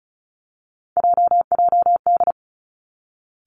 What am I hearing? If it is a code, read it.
Morse code, heard as a keyed tone, about 36 wpm, 700 Hz. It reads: JJD
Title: Three Challenging Letter-Number Combinations 35wpm